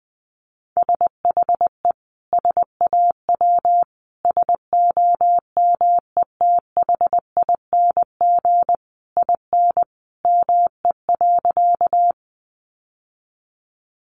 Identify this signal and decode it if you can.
Morse code, heard as a keyed tone, about 20 wpm, 700 Hz. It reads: SHE SAW SOMETHING IN ME.
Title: Sentences from Top 600 Words 20wpm